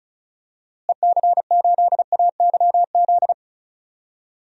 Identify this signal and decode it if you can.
Morse code, heard as a keyed tone, about 34 wpm, 700 Hz. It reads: EC8AYZ
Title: DXCC-100 Call Signs - 6 Characters - Repeated 3x - 35wpm